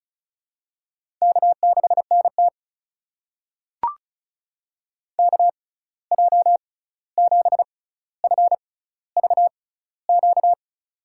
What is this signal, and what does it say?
Morse code, heard as a keyed tone, about 36 wpm, spaced as for 16 wpm, 700 Hz. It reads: K6NT KJ7FVQ
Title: USA Call Signs - 6x Character Spacing - 35wpm